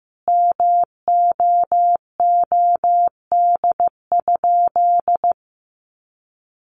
Morse code, heard as a keyed tone, about 15 wpm, 700 Hz. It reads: MOOD?
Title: Sentences from Top 1100 Words 15wpm